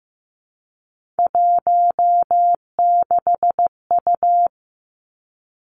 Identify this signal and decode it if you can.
Morse code, heard as a keyed tone, about 15 wpm, 700 Hz. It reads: 16U
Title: Three Challenging Letter-Number Combinations 15wpm